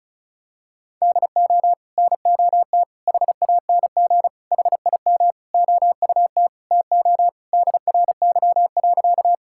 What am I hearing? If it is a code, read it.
Morse code, heard as a keyed tone, about 34 wpm, 700 Hz. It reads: DO NOT HANG HIM OUT TO DRY.
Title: Sentences from Top 1600 Words 35wpm